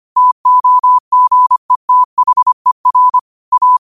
Morse code, heard as a keyed tone, about 25 wpm, 1000 Hz. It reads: TOGETHER A
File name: qotdfiles / 25wpm